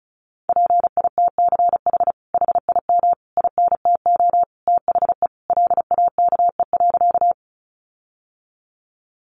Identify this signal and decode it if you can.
Morse code, heard as a keyed tone, about 34 wpm, 700 Hz. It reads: PITCH HIM INTO THE LAKE.
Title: Sentences from Top 1300 Words 35wpm